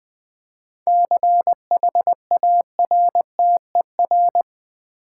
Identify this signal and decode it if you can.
Morse code, heard as a keyed tone, about 20 wpm, 700 Hz. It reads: CHARTER
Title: 7 Letter Words - First 3 Letters Given - 20wpm